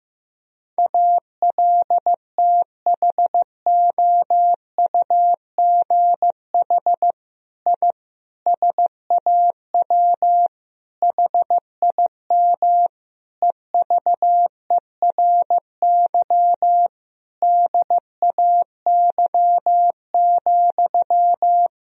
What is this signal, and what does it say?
Morse code, heard as a keyed tone, about 15 wpm, 700 Hz. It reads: ALTHOUGH I SAW HIM EVERY DAY,
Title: Sentences from Top 1000 Words - Review Part 3 - 15wpm